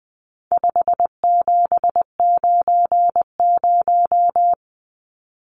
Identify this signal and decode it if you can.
Morse code, heard as a keyed tone, about 20 wpm, 700 Hz. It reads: 5790
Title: Numbers - 4-Digit 20wpm